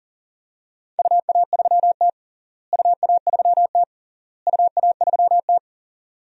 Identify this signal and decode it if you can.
Morse code, heard as a keyed tone, about 40 wpm, 700 Hz. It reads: UA3T UA3T UA3T
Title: DXCC-100 Call Signs - 4 Characters - Repeated 3x - 40wpm